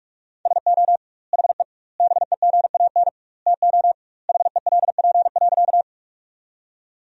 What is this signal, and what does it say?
Morse code, heard as a keyed tone, about 46 wpm, 700 Hz. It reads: SO HE BEGAN TO HELP.
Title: 2x Word-Spacing - Sentences from Top 300 Words - 45wpm